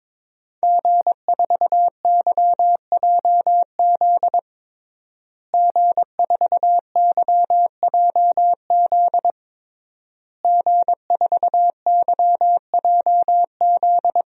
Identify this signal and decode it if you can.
Morse code, heard as a keyed tone, about 22 wpm, 700 Hz. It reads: G4YJZ G4YJZ G4YJZ
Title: DXCC-100 Call Signs - 5 Characters - Repeated 3x - 22wpm